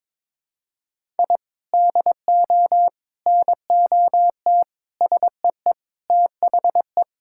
Morse code, heard as a keyed tone, about 22 wpm, 700 Hz. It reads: I DO NOT SEE THE
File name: practice / 22wpm / Sentences with 4-Word Phrases - 22wpm